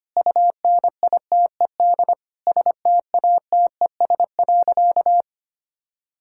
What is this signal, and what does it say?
Morse code, heard as a keyed tone, about 25 wpm, 700 Hz. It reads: UNITED STATES.
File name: practice / 25wpm / Sentences with 9-Word Phrases - 25wpm